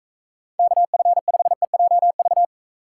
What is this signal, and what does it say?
Morse code, heard as a keyed tone, about 42 wpm, 700 Hz. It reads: KF5EJV